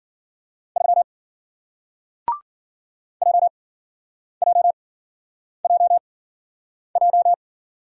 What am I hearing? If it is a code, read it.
Morse code, heard as a keyed tone, about 54 wpm, spaced as for 9 wpm, 700 Hz. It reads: 3 JJJJ